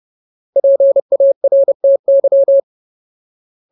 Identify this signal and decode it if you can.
Morse code, heard as a keyed tone, about 30 wpm, 550 Hz. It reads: PARTY
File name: Ultimate Word List / 30WPM Ultimate